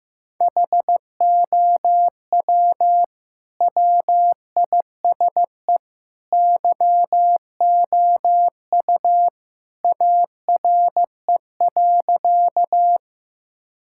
Morse code, heard as a keyed tone, about 15 wpm, 700 Hz. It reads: HOW WISE YOU ARE.
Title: Sentences from Top 1600 Words 15wpm